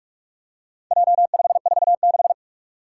Morse code, heard as a keyed tone, about 44 wpm, 700 Hz. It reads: J546
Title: Four Challenging Letter-Number Combinations 45wpm